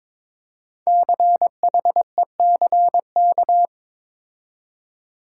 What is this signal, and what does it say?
Morse code, heard as a keyed tone, about 23 wpm, 700 Hz. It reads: CHECK
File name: 5 Letter Words - First 3 Letters Given - 22wpm